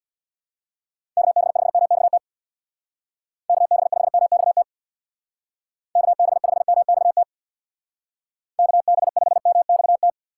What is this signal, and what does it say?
Morse code, heard as a keyed tone, about 64 wpm, 700 Hz. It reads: KB5MXT KB5MXT KB5MXT KB5MXT